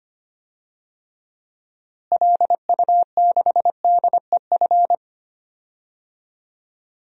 Morse code, heard as a keyed tone, about 25 wpm, 700 Hz. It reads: LU6DEF
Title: DXCC-100 Call Signs - 6 Characters - Repeated 2x - 25wpm